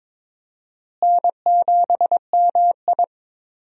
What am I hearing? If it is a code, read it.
Morse code, heard as a keyed tone, about 22 wpm, 700 Hz. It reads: N7MI
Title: USA Call Signs - 4 Characters - Repeated 2x - 22wpm